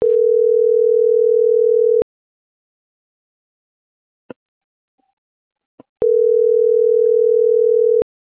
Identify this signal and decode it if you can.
Ringback tone (North America), 440 + 480 Hz.